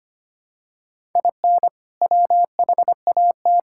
Morse code, heard as a keyed tone, about 25 wpm, 700 Hz. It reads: IN WHAT
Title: Sentences from Top 1000 Words - Review Part 5 - 25wpm